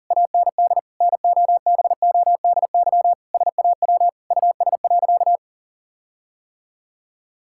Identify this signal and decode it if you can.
Morse code, heard as a keyed tone, about 40 wpm, 700 Hz. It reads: AND NOBODY SAW US.
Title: Sentences from Top 1600 Words 40wpm